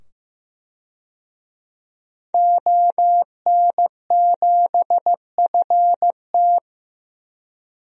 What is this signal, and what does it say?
Morse code, heard as a keyed tone, about 15 wpm, 700 Hz. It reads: ON7FT
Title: DXCC-100 Call Signs - 5 Characters - 15wpm